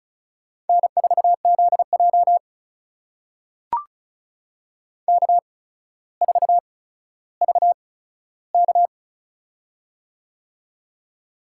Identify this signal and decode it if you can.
Morse code, heard as a keyed tone, about 36 wpm, spaced as for 15 wpm, 700 Hz. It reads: N4ZJ K4VK
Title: USA Call Signs - 4 Characters - 8x Character Spacing - 35wpm